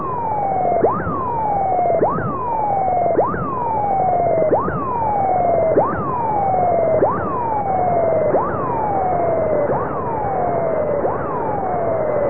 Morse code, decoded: none